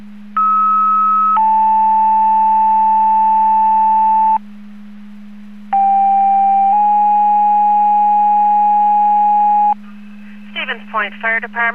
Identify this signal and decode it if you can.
Two-tone page, heard as a one-second tone, then a three-second tone, about 1290 then 880 Hz, about 790 then 830 Hz, with speech. It is heard from 0.4 to 9.7 s.